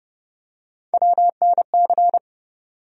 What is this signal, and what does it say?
Morse code, heard as a keyed tone, about 30 wpm, 700 Hz. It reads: WNC
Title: USA Call Sign Postfixes - 3 Letters 30wpm